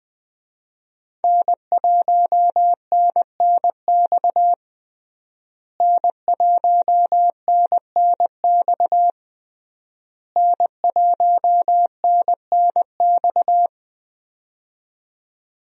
Morse code, heard as a keyed tone, about 20 wpm, 700 Hz. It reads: N1NNX N1NNX N1NNX
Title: USA Call Signs - 5 Characters - Repeated 3x - 20wpm